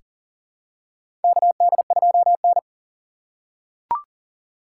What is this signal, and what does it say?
Morse code, heard as a keyed tone, about 38 wpm, 700 Hz. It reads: KD2N
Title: USA Call Signs - 4 Characters - 2x Character Spacing - 40wpm